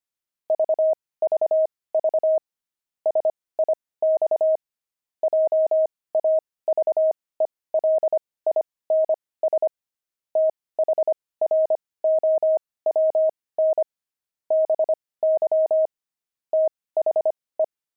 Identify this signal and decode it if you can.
Morse code, heard as a keyed tone, about 25 wpm, spaced as for 18 wpm, 630 Hz. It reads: VVV SIX JAVELINS THROWN BY THE
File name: pangram-fs-18-cs-25-javelins